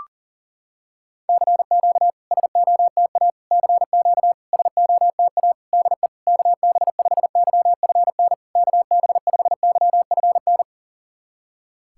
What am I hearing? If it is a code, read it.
Morse code, heard as a keyed tone, about 40 wpm, 700 Hz. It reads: CQ SOTA CQ SOTA DE KB5YFN KB5YFN
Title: SOTA QSO Practice 40wpm